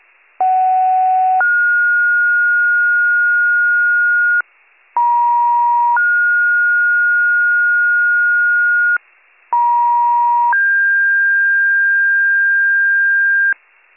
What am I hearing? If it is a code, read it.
Two-tone page, heard as a one-second tone, then a three-second tone, about 750 then 1430 Hz, about 950 then 1430 Hz, about 950 then 1600 Hz, with noise. It is heard from 0.4 to 13.5 s.